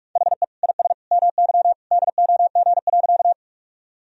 Morse code, heard as a keyed tone, about 46 wpm, 700 Hz. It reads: HE IS MY DOG.
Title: Sentences from Top 1000 Words - Review Part 2 - 45wpm